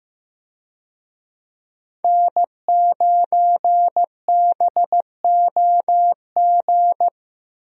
Morse code, heard as a keyed tone, about 15 wpm, 700 Hz. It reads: N9BOG
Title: USA Call Signs - 5 Characters - Repeated 3x - 15wpm